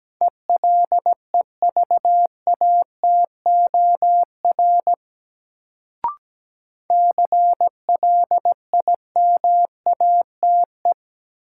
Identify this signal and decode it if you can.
Morse code, heard as a keyed tone, about 17 wpm, 700 Hz. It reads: ELEVATOR CLIMATE
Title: Sentences from Top 1600 Words - New Word Drill - 17wpm